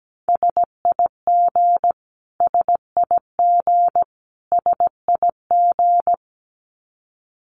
Morse code, heard as a keyed tone, about 17 wpm, 700 Hz. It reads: SIG SIG SIG
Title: Abbreviations Repeated 3x - 17wpm